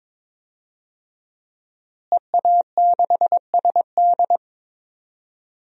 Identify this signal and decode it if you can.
Morse code, heard as a keyed tone, about 22 wpm, 700 Hz. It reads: EA6SD
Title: DXCC-100 Call Signs - 5 Characters - Repeated 3x - 22wpm